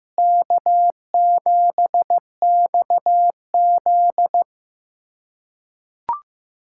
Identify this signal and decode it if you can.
Morse code, heard as a keyed tone, about 16 wpm, 700 Hz. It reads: K7XZ
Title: USA Callsign DE Callsign - 4 Characters - 15wpm